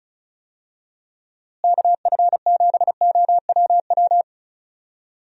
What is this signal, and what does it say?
Morse code, heard as a keyed tone, about 36 wpm, 700 Hz. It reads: KF7OWW
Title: USA Call Signs - 6 Characters - Repeated 3x - 35wpm